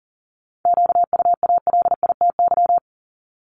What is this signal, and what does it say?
Morse code, heard as a keyed tone, about 42 wpm, 700 Hz. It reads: QUALITY
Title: Multi-letter Phonemes 40wpm